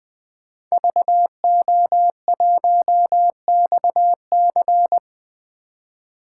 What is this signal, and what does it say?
Morse code, heard as a keyed tone, about 20 wpm, 700 Hz. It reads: VO1XC
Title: K1USN SST QSO Practice - 20wpm